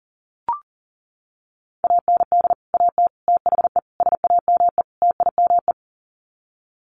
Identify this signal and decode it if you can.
Morse code, heard as a keyed tone, about 37 wpm, 700 Hz. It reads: AND AT THE SAME TIME
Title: Sets of 5 Words 40wpm